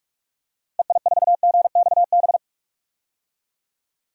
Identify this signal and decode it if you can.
Morse code, heard as a keyed tone, about 46 wpm, 700 Hz. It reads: EI4GXB
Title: DXCC-100 Call Signs - 6 Characters - 45wpm